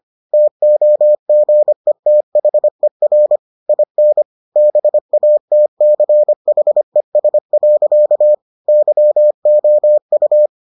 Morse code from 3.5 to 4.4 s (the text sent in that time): IN